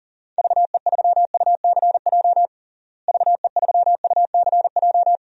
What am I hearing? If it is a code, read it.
Morse code, heard as a keyed tone, about 40 wpm, 700 Hz. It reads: VE3UCJ VE3UCJ
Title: DXCC-100 Call Signs - 6 Characters - Repeated 3x - 40wpm